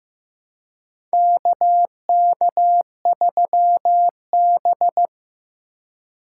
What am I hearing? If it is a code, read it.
Morse code, heard as a keyed tone, about 15 wpm, 700 Hz. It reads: KK3B
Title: USA Call Signs - 4 Characters - 2x Character Spacing - 15wpm